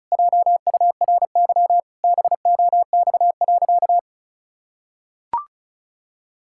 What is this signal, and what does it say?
Morse code, heard as a keyed tone, about 34 wpm, 700 Hz. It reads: JURY BOX.
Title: Pangram Challenge v2 - 35wpm